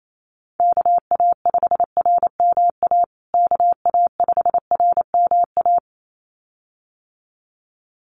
Morse code, heard as a keyed tone, about 28 wpm, 700 Hz. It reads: KA5RMA KA5RMA
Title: USA Call Signs Repeated 2x 28wpm